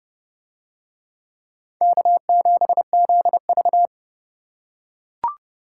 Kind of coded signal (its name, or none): Morse code